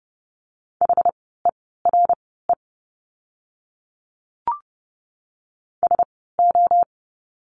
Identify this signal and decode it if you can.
Morse code, heard as a keyed tone, about 30 wpm, spaced as for 17 wpm, 700 Hz. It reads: HERE SO
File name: course / 30:10wpm / 115 Lesson 19f - Letter w - Words 30wpm